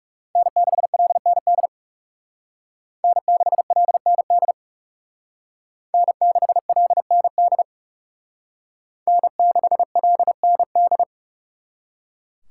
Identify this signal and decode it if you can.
Morse code, heard as a keyed tone, about 36 wpm, 700 Hz. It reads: N6LND N6LND N6LND N6LND
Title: Speed Racing - USA 1x3 Call Signs 30wpm